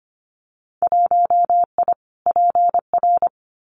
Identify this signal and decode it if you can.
Morse code, heard as a keyed tone, about 25 wpm, 700 Hz. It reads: 1I PR
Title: Winter Field Day Exchanges Encore 25wpm